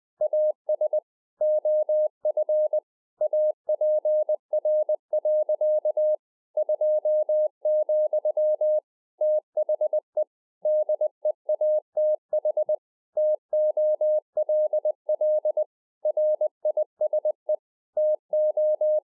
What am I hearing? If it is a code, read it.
Morse code, heard as a keyed tone, about 20 wpm, 610 Hz. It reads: AS OF APR. 2, THE DEATH TOLL RISE TO